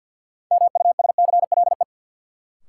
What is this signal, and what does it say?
Morse code, heard as a keyed tone, about 53 wpm, 700 Hz. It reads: MUSCLE
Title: Sentences from Top 1300 Words - New Word Drill - 50wpm